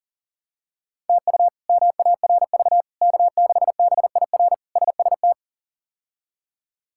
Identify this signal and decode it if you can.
Morse code, heard as a keyed tone, about 40 wpm, 700 Hz. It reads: TU MARV K6BIR SST
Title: K1USN SST QSO Practice - 40wpm